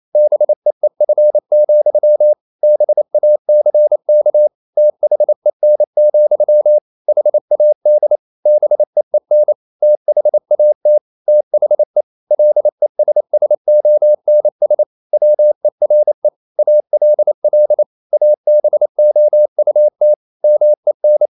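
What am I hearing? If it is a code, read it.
Morse code, heard as a keyed tone, about 28 wpm, 600 Hz. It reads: BEEF, BACK THEN, HAD BEEN THAT THE LESSONS WERE ALL ABOUT MEN
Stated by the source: Morse code practice at 28 wpm, text of Ken Follett's novel Never in English, from CW med Gnister.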